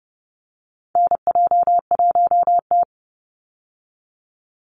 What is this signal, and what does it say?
Morse code, heard as a keyed tone, about 30 wpm, 700 Hz. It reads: NJ1T
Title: Call Signs Repeated 3x - 30wpm